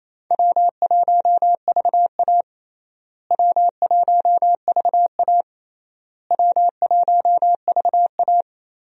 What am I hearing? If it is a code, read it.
Morse code, heard as a keyed tone, about 28 wpm, 700 Hz. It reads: W1VA W1VA W1VA
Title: USA Call Signs - 4 Characters - Repeated 3x - 28wpm